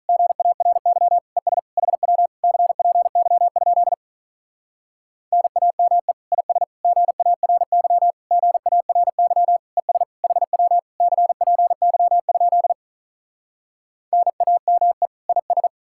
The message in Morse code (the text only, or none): GARY ES HW CPY? NAME IS GARY GARY ES HW CPY? NAME IS